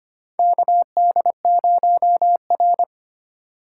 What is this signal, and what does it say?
Morse code, heard as a keyed tone, about 25 wpm, 700 Hz. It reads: KD0R
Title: USA Call Signs - 4 Characters - 3x Character Spacing - 25wpm